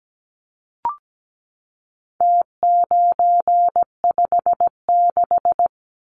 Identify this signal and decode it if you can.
Morse code, heard as a keyed tone, about 17 wpm, 700 Hz. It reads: T956
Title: Numbers - Contest Style - 4-Digit 17wpm